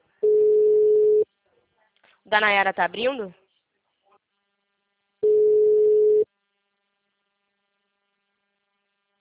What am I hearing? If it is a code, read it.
Ringback tone (Europe, 425 Hz), with speech.
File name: out- -2008- - - .8